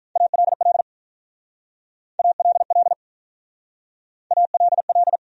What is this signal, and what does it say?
Morse code, heard as a keyed tone, about 44 wpm, 700 Hz. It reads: ALL ALL ALL